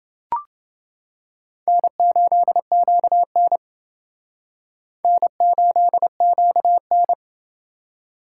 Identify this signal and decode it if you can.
Morse code, heard as a keyed tone, about 29 wpm, 700 Hz. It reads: N8QN N8QN